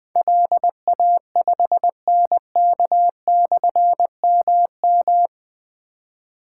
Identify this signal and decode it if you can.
Morse code, heard as a keyed tone, about 20 wpm, 700 Hz. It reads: LA5NK/MM
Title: Difficult Call Signs 20wpm